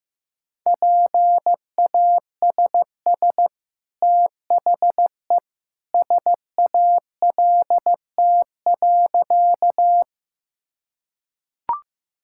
Morse code, heard as a keyed tone, about 15 wpm, 700 Hz. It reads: PASS THE SALT.
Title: Sentences from Top 1000 Words - Review Part 5 - 15wpm